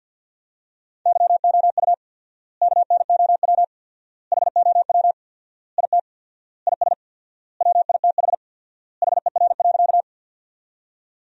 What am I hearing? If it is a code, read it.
Morse code, heard as a keyed tone, about 50 wpm, 700 Hz. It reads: YOU KNOW HOW IT IS WITH HER.
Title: 4x Word-Spacing - Sentences from Top 100 Words - 50wpm